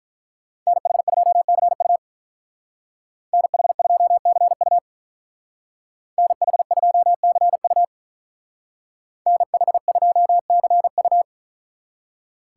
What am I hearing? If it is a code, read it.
Morse code, heard as a keyed tone, about 46 wpm, 700 Hz. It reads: NH2CU NH2CU NH2CU NH2CU